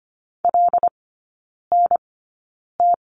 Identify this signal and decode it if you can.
Morse code, heard as a keyed tone, about 25 wpm, spaced as for 9 wpm, 690 Hz. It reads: LNT
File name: Lesson 15e - Number 5 - Three Characters 25wpm